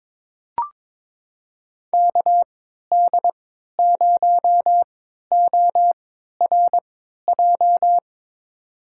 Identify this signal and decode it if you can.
Morse code, heard as a keyed tone, about 22 wpm, spaced as for 13 wpm, 700 Hz. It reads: KD0ORJ
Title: USA Call Signs - 3x Character Spacing - 22wpm